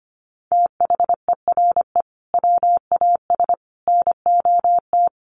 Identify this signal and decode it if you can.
Morse code, heard as a keyed tone, about 25 wpm, 700 Hz. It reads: THERE WAS NOT
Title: Sentences from Top 1400 Words 25wpm